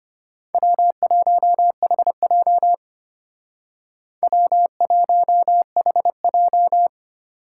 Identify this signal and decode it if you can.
Morse code, heard as a keyed tone, about 31 wpm, 700 Hz. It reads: W1HJ W1HJ
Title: Speed Racing - USA 1x2 Call Signs 25wpm